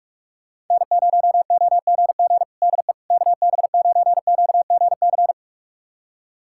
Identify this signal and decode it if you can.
Morse code, heard as a keyed tone, about 41 wpm, 700 Hz. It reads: N0OGG DE KB9QGC
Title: USA Callsign DE Callsign 45wpm